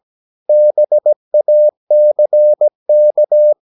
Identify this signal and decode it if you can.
Morse code, heard as a keyed tone, about 17 wpm, 600 Hz. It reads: BACK
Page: Romanen Never af Ken Follet på engelsk.